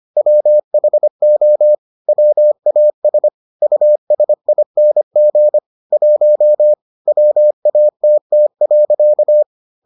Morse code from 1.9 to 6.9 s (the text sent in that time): WAS USING 1